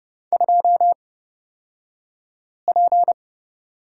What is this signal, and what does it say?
Morse code, heard as a keyed tone, about 30 wpm, spaced as for 5 wpm, 690 Hz. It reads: 2P
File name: Lesson 36d - BT - Two Characters 30wpm